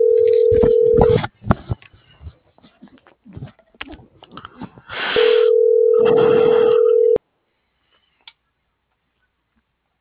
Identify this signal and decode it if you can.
Ringback tone (North America), 440 + 480 Hz.